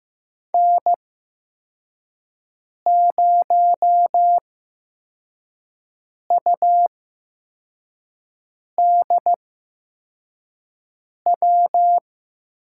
Morse code, heard as a keyed tone, about 16 wpm, spaced as for 4 wpm, 700 Hz. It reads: N0UDW
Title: USA Call Signs - 5 Characters - 8x Character Spacing - 15wpm